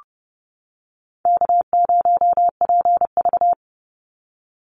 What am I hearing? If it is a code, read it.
Morse code, heard as a keyed tone, about 30 wpm, 700 Hz. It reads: K0PV
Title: Easy Call Signs - 30wpm